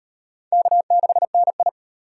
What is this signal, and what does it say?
Morse code, heard as a keyed tone, about 37 wpm, 700 Hz. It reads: K6NI